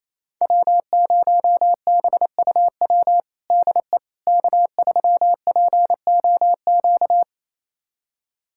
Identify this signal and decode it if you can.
Morse code, heard as a keyed tone, about 28 wpm, 700 Hz. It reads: W0BUW DE K3POQ
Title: USA Callsign DE Callsign - 5 Characters - 28wpm